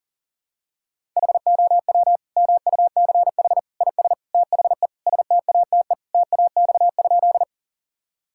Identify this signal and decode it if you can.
Morse code, heard as a keyed tone, about 40 wpm, 700 Hz. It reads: HOW MUCH IS THE STATE TAX?
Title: Sentences from Top 1000 Words - Review Part 5 - 40wpm